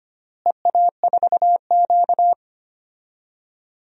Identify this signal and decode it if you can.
Morse code, heard as a keyed tone, about 25 wpm, 700 Hz. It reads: EA4Q
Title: DXCC-100 Call Signs - 4 Characters - Repeated 2x - 25wpm